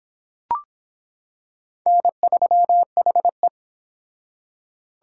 Morse code, heard as a keyed tone, about 25 wpm, 700 Hz. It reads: N3HE